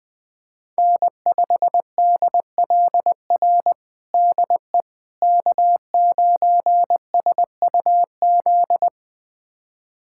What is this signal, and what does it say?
Morse code, heard as a keyed tone, about 21 wpm, 700 Hz. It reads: N5DLR DE K9SUZ
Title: USA Callsign DE Callsign - 5 Characters - 20wpm